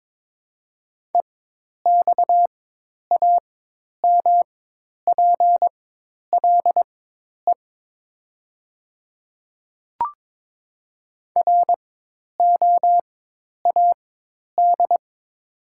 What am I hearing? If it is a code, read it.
Morse code, heard as a keyed tone, about 22 wpm, spaced as for 10 wpm, 700 Hz. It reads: EXAMPLE ROAD
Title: 4x Character Spacing - Top 500 Words - 22wpm